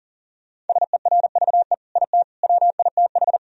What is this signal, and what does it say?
Morse code, heard as a keyed tone, about 40 wpm, 700 Hz. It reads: SERVE IT WITH
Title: Sentences from Top 1000 Words - Review Part 3 - 40wpm